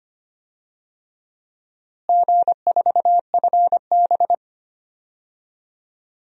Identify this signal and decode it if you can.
Morse code, heard as a keyed tone, about 25 wpm, 700 Hz. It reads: G4FB
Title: DXCC-100 Call Signs - 4 Characters - Repeated 3x - 25wpm